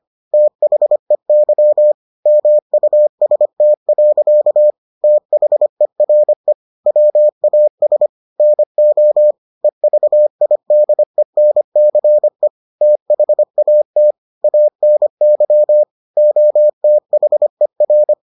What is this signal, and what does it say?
Morse code, heard as a keyed tone, about 25 wpm, 600 Hz. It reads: THEY MUST. THERE WAS NO EVIDENCE THAT ANY OTHER